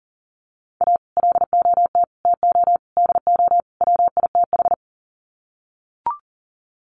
Morse code, heard as a keyed tone, about 40 wpm, 700 Hz. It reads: A LOT TO DO WITH
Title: Sets of 5 Words 40wpm